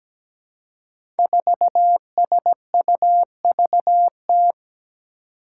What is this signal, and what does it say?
Morse code, heard as a keyed tone, about 17 wpm, 700 Hz. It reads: 4SUVT
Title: Five Challenging Letter-Number Combinations 17wpm